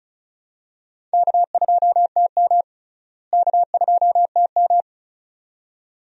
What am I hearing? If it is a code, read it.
Morse code, heard as a keyed tone, about 36 wpm, 700 Hz. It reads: K2TM K2TM
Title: USA Call Signs - 4 Characters - Repeated 2x - 35wpm